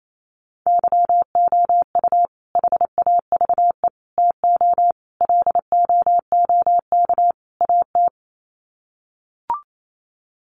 Morse code, heard as a keyed tone, about 28 wpm, 700 Hz. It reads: YOU HAVE TO LOOK AT
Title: Sets of 5 Words Encore 28wpm